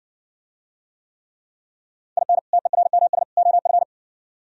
Morse code, heard as a keyed tone, about 54 wpm, 700 Hz. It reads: IN TERMS OF